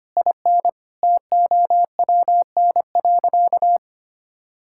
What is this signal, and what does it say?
Morse code, heard as a keyed tone, about 25 wpm, 700 Hz. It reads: IN TOWN.
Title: Sentences from Top 1000 Words - Review Part 6 - 25wpm